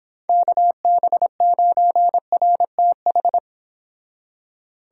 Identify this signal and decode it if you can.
Morse code, heard as a keyed tone, about 29 wpm, 700 Hz. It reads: KB9RTH